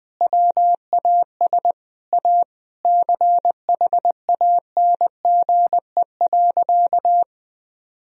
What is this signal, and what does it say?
Morse code, heard as a keyed tone, about 21 wpm, 700 Hz. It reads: WAS A CHANGE.
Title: Sentences from Top 1000 Words - Review Part 1 - 20wpm